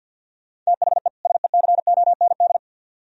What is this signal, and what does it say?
Morse code, heard as a keyed tone, about 64 wpm, 700 Hz. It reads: THE SECOND